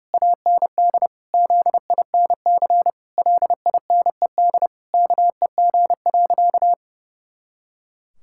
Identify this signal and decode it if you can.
Morse code, heard as a keyed tone, about 30 wpm, 700 Hz. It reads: AND ZINC LINED KEG.
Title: Pangram Challenge v2 - 30wpm